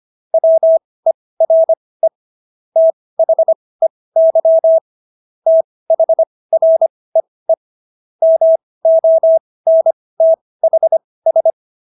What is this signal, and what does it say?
Morse code, heard as a keyed tone, about 25 wpm, spaced as for 18 wpm, 650 Hz. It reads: WERE THEY THREE MONTHS